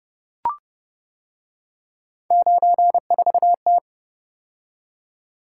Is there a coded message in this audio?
Yes: Morse code from 2.3 to 3.8 s, heard as a keyed tone, about 30 wpm, 700 Hz. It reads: 94T